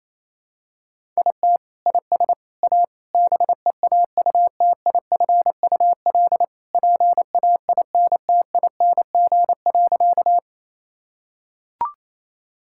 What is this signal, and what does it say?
Morse code, heard as a keyed tone, about 28 wpm, 700 Hz. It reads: IT IS A BEAUTIFUL PAINTING.
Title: Sentences from Top 1000 Words - Review Part 6 - 28wpm